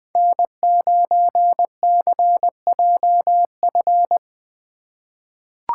Morse code, heard as a keyed tone, about 20 wpm, 700 Hz. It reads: N9CJF
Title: USA Call Signs - 5 Characters - 2x Character Spacing - 20wpm